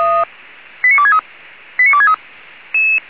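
Five-tone selective call (ZVEI), heard as quick runs of five tones, tones of about 70 ms. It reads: 89263 89263
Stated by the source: ZVEI five-tone selective call received on an SDR.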